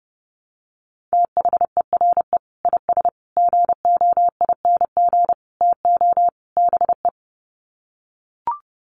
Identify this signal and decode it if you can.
Morse code, heard as a keyed tone, about 30 wpm, 700 Hz. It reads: THERE IS GOING TO BE
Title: Sets of 5 Words Encore 30wpm